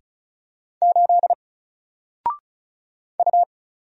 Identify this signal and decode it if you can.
Morse code, heard as a keyed tone, about 33 wpm, 700 Hz. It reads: 8 U
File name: Single-Letter-Number-35wpm